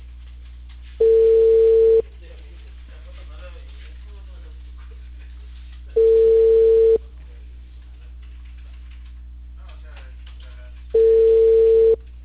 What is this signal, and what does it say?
Ringback tone (China, 450 Hz), with speech.